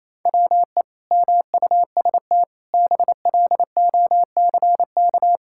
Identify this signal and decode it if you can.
Morse code, heard as a keyed tone, about 28 wpm, 700 Hz. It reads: WE MUST BLOCK
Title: Sentences from Top 1600 Words 28wpm